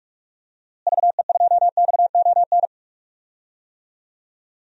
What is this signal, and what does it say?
Morse code, heard as a keyed tone, about 46 wpm, 690 Hz. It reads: VE2XON
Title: DXCC-100 Call Signs - 6 Characters - Repeated 2x - 45wpm